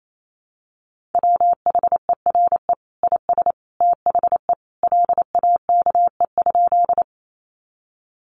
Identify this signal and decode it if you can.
Morse code, heard as a keyed tone, about 28 wpm, 700 Hz. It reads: WHERE IS THE LAKE?
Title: Sentences from Top 1100 Words 28wpm